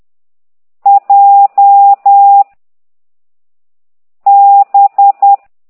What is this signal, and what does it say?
Morse code, heard as a keyed tone, about 10 wpm, 800 Hz. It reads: J B
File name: ct2idw / exerc6